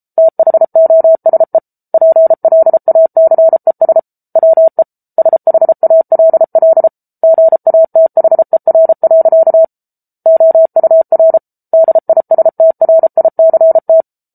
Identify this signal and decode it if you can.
Morse code, heard as a keyed tone, about 34 wpm, 650 Hz. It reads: THOSE PLACES WE SHALL GATHER. OUR DISTRICT